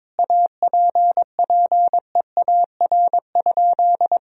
Morse code, heard as a keyed tone, about 22 wpm, 700 Hz. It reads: APPEAR?
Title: Sentences from Top 1000 Words - Review Part 2 - 22wpm